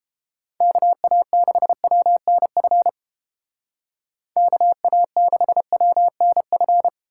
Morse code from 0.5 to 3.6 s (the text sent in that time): KA6WNF